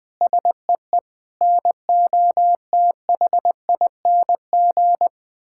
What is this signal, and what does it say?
Morse code, heard as a keyed tone, about 20 wpm, 700 Hz. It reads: SEE NOTHING
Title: Sentences from Top 1000 Words - Review Part 6 - 20wpm